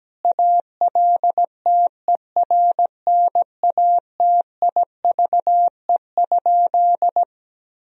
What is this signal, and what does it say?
Morse code, heard as a keyed tone, about 17 wpm, 700 Hz. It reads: ALTERNATIVE?
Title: Sentences from Top 1000 Words - Review Part 4 - 17wpm